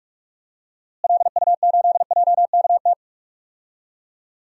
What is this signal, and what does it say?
Morse code, heard as a keyed tone, about 42 wpm, 700 Hz. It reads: LU8JKT
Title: DXCC-100 Call Signs - 6 Characters - Repeated 3x - 45wpm